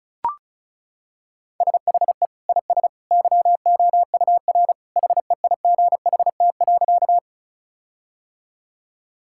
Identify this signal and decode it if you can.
Morse code, heard as a keyed tone, about 36 wpm, 700 Hz. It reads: SHE IS YOUR HEIGHT.
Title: Sentences from Top 1600 Words 35wpm